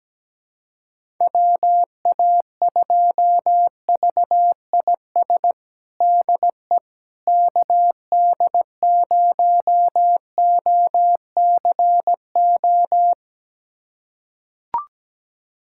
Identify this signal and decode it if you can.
Morse code, heard as a keyed tone, about 17 wpm, 700 Hz. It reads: WA2VIS DE KD0OCO
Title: USA Callsign DE Callsign - 6 Characters - 17wpm